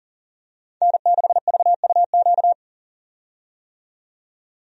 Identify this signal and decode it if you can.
Morse code, heard as a keyed tone, about 40 wpm, 700 Hz. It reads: N6VUQ
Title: USA Call Signs - 5 Characters - 40wpm